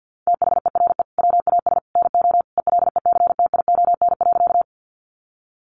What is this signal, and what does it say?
Morse code, heard as a keyed tone, about 50 wpm, 700 Hz. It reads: THERE WAS NO ELECTION.
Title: Sentences from Top 1500 Words 50wpm